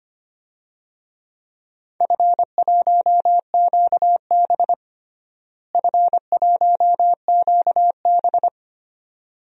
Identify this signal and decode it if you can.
Morse code, heard as a keyed tone, about 25 wpm, 700 Hz. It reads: F1QB F1QB
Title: DXCC-100 Call Signs - 4 Characters - Repeated 2x - 25wpm